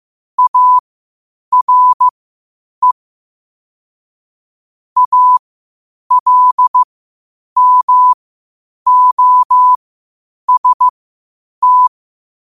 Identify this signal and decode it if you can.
Morse code, heard as a keyed tone, about 15 wpm, spaced as for 8 wpm, 1000 Hz. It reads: ARE ALMOST
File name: qotdfiles / 8wpm